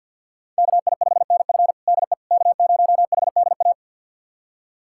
Morse code, heard as a keyed tone, about 53 wpm, 700 Hz. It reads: KI5NF DE K0HDA
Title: USA Callsign DE Callsign - 5 Characters - 50wpm